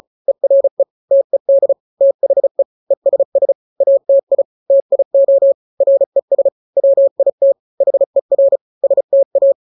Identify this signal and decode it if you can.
Morse code, heard as a keyed tone, about 34 wpm, 550 Hz. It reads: ERE TED THE ESS ATI TIO RES WIT HER STA
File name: 3-grams 35 30 WPM